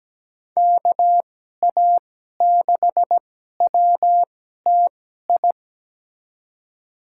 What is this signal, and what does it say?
Morse code, heard as a keyed tone, about 17 wpm, spaced as for 12 wpm, 700 Hz. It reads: KA6WTI
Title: USA Call Signs - 2x Character Spacing - 17wpm